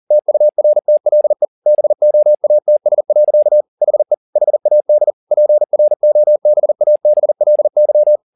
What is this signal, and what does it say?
Morse code, heard as a keyed tone, about 40 wpm, 600 Hz. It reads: TURTLE BOATS. HE HAD PROBABLY